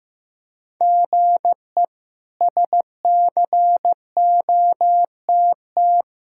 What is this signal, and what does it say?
Morse code, heard as a keyed tone, about 16 wpm, 700 Hz. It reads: GE SCOTT
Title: K1USN SST QSO Practice - 15wpm